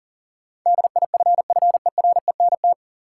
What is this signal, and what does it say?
Morse code, heard as a keyed tone, about 40 wpm, 700 Hz. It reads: DIFFERENT
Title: Sentences from Top 600 Words - New Word Drill - 40wpm